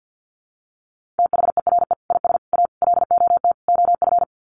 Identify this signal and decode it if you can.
Morse code, heard as a keyed tone, about 50 wpm, 700 Hz. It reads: THERE IS A LOT OF
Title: Sets of 5 Words 50wpm